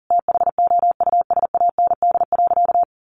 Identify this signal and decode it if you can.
Morse code, heard as a keyed tone, about 40 wpm, 700 Hz. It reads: THOUSAND.
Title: Sentences from Top 1300 Words 40wpm